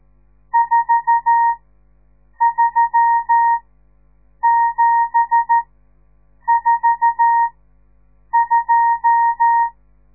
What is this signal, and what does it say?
Morse code, heard as a keyed tone, about 13 wpm, spaced as for 7 wpm, 930 Hz. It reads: 43742
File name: radiomorningstar / morse / les 10